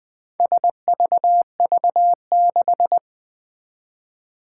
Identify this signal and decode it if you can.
Morse code, heard as a keyed tone, about 20 wpm, 700 Hz. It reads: SVV6
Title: Four Challenging Letter-Number Combinations 20wpm